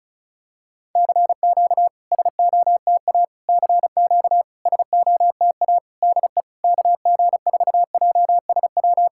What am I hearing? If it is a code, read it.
Morse code, heard as a keyed tone, about 34 wpm, 700 Hz. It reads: CQ SOTA CQ SOTA DE KG4JSW
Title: SOTA QSO Practice 35wpm